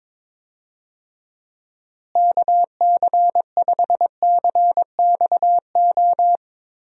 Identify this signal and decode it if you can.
Morse code, heard as a keyed tone, about 21 wpm, 700 Hz. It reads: KC5CXO
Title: USA Call Signs - 6 Characters - Repeated 2x - 22wpm